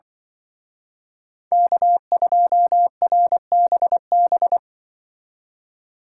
Morse code, heard as a keyed tone, about 25 wpm, 700 Hz. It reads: K2RBB
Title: Speed Racing - USA 1x3 Call Signs 20wpm